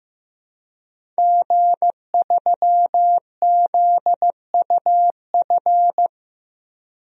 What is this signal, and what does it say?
Morse code, heard as a keyed tone, about 15 wpm, 700 Hz. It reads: G3ZUF
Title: DXCC-100 Call Signs - 5 Characters - Repeated 2x - 15wpm